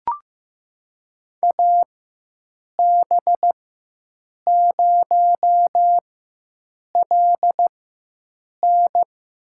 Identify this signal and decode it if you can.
Morse code, heard as a keyed tone, about 15 wpm, spaced as for 7 wpm, 700 Hz. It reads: AB0LN
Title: USA Call Signs - 5 Characters - 4x Character Spacing - 15wpm